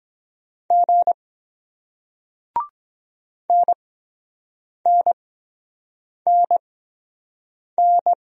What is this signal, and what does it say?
Morse code, heard as a keyed tone, about 23 wpm, spaced as for 7 wpm, 700 Hz. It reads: G NNNN